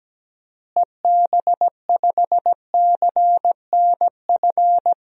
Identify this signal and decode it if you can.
Morse code, heard as a keyed tone, about 17 wpm, 700 Hz. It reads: EB5CNF
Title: DXCC-100 Call Signs - 6 Characters - Repeated 2x - 17wpm